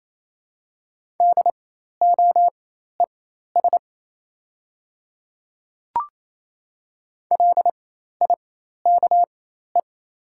Morse code, heard as a keyed tone, about 28 wpm, spaced as for 13 wpm, 700 Hz. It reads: DOES LIKE
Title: 4x Character Spacing - Top 500 Words - 28wpm